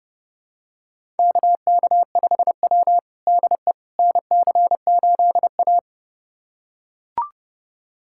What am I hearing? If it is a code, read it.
Morse code, heard as a keyed tone, about 30 wpm, 700 Hz. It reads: KK5W DE NC8A
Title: USA Callsign DE Callsign - 4 Characters - 30wpm